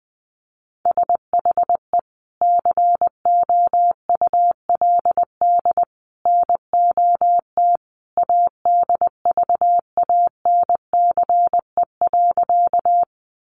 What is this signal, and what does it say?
Morse code, heard as a keyed tone, about 19 wpm, 700 Hz. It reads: SHE COULD NOT ADVANCE.
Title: Sentences from Top 1400 Words 20wpm